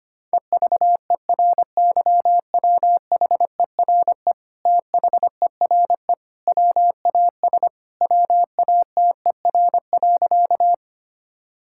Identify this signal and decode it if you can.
Morse code, heard as a keyed tone, about 25 wpm, 700 Hz. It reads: EVERYWHERE THERE WAS WATER.
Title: Sentences from Top 1600 Words 25wpm